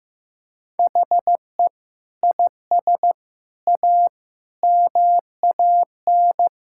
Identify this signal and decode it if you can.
Morse code, heard as a keyed tone, about 15 wpm, 700 Hz. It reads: HE IS A MAN
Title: Sentences from Top 1000 Words - Review Part 6 - 15wpm